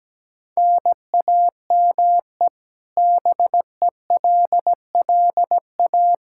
Morse code, heard as a keyed tone, about 17 wpm, 700 Hz. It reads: NAME BELLA
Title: Practice QSO - Exchange 1 - Beginning - 17wpm